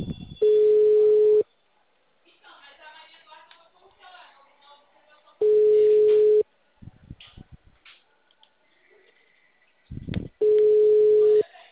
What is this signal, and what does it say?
Ringback tone (Europe, 425 Hz).